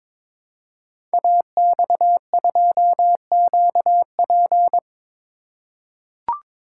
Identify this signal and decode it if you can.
Morse code, heard as a keyed tone, about 22 wpm, 700 Hz. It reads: AX2QP
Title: Code Groups - 5 Letter:Numbers - 22wpm